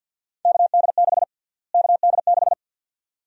Morse code, heard as a keyed tone, about 46 wpm, 700 Hz. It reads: KD6 KD6
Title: USA Call Sign Prefixes - Repeated 2x - 50wpm